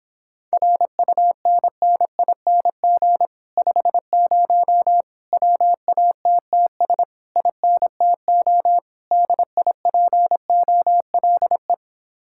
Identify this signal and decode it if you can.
Morse code, heard as a keyed tone, about 26 wpm, 700 Hz. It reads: RUNNING 50 WATTS INTO DIPOLE